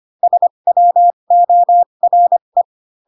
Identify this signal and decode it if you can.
Morse code, heard as a keyed tone, about 25 wpm, spaced as for 22 wpm, 700 Hz. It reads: SWORE
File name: Pinocchio / Pinocchio ch2 18 25 ik1hge